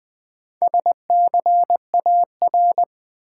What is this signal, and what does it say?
Morse code, heard as a keyed tone, about 20 wpm, 700 Hz. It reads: SCAR
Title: Build A Word - 2 Letters At A Time - 6-Letter Words - 20wpm